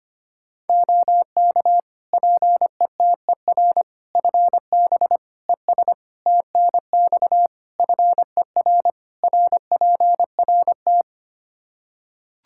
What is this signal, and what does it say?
Morse code, heard as a keyed tone, about 25 wpm, 700 Hz. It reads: OK PETER FB ES TNX FER RPRT
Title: K1USN SST QSO Practice 25wpm